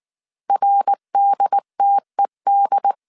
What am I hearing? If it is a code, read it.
Morse code, heard as a keyed tone, about 19 wpm, 800 Hz. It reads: RDTED
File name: CW / DET / 1 3 1500C S B 18wpm 1916